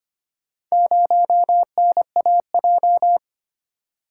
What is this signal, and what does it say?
Morse code, heard as a keyed tone, about 25 wpm, 700 Hz. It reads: 0NAJ
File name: practice / 25wpm / Four Letter-Number Combinations - 25wpm